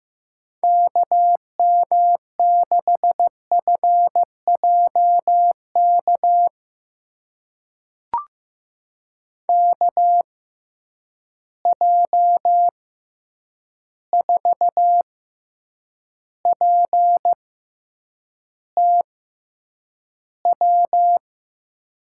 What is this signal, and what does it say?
Morse code, heard as a keyed tone, about 15 wpm, spaced as for 8 wpm, 700 Hz. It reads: KM6FJK KJ4PTW
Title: USA Call Signs - 6x Character Spacing - 15wpm